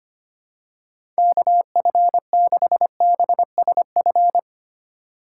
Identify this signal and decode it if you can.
Morse code, heard as a keyed tone, about 25 wpm, 700 Hz. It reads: KF6BSF
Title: USA Call Signs - 3x Character Spacing - 25wpm